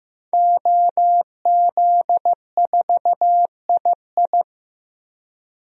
Morse code, heard as a keyed tone, about 15 wpm, 700 Hz. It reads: OZ4II
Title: DXCC-100 Call Signs - 5 Characters - 15wpm